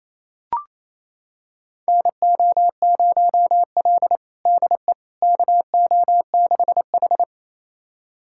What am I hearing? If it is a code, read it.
Morse code, heard as a keyed tone, about 28 wpm, 700 Hz. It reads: NO0L DE KO6H
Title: USA Callsign DE Callsign - 4 Characters - 28wpm